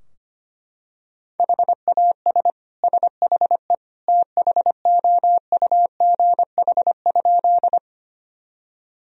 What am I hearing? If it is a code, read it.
Morse code, heard as a keyed tone, about 25 wpm, 700 Hz. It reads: HAS SHE THOUGH?
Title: Sentences from Top 1000 Words - Review Part 2 - 25wpm